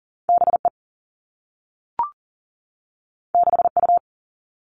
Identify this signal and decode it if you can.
Morse code, heard as a keyed tone, about 40 wpm, 700 Hz. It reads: BE 6U
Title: Two Challenging Letter-Number Combinations 40wpm